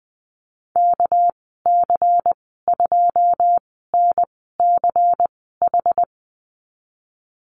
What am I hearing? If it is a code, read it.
Morse code, heard as a keyed tone, about 20 wpm, spaced as for 15 wpm, 700 Hz. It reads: KC2NCH
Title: 200 Lesson 32g - Letter k - Call Signs 20wpm